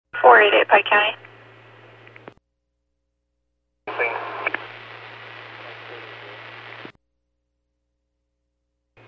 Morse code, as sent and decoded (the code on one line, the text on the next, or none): none